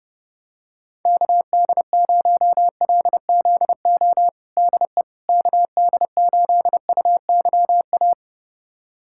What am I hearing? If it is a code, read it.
Morse code, heard as a keyed tone, about 30 wpm, 700 Hz. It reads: KD0LZO DE KD8UYA
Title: USA Callsign DE Callsign - 6 Characters - 30wpm